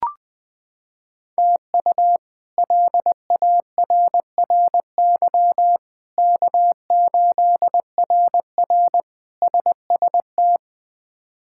Morse code, heard as a keyed tone, about 20 wpm, 700 Hz. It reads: TU LARRY K8RR SST
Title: K1USN SST QSO Practice - 20wpm